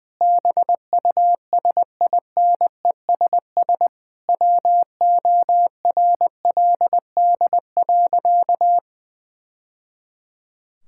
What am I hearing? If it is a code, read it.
Morse code, heard as a keyed tone, about 20 wpm, 700 Hz. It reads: BUSINESS WORLD.
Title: Sentences from Top 1000 Words - Review Part 6 - 20wpm